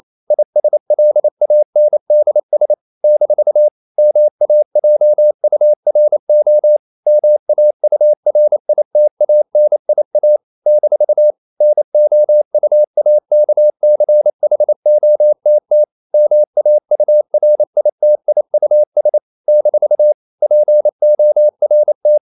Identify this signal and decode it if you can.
Morse code, heard as a keyed tone, about 28 wpm, 600 Hz. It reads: ISLANDS - MAJURO MAURITANIA - NOUAKCHOTT MAURITIUS - PORT
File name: CW / Practice / NationalCapitals28